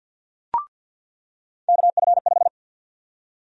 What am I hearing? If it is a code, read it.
Morse code, heard as a keyed tone, about 47 wpm, 700 Hz. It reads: KF5
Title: USA Call Sign Prefixes - 50wpm